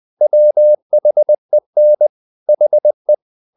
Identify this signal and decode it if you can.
Morse code, heard as a keyed tone, about 20 wpm, 600 Hz. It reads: WHEN HE
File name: the old man and the sea part 2 20wpm